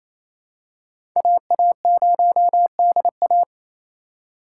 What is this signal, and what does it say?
Morse code, heard as a keyed tone, about 28 wpm, 700 Hz. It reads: AA0DA
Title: USA Call Signs - 5 Characters - 2x Character Spacing - 28wpm